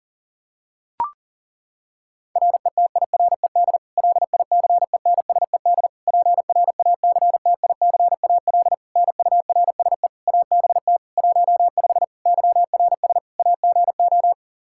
Morse code, heard as a keyed tone, about 40 wpm, 700 Hz. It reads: RETIRED LICENSED PRACTICAL NURSE ABT 15 YRS AGO
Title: K1USN SST QSO Practice 40wpm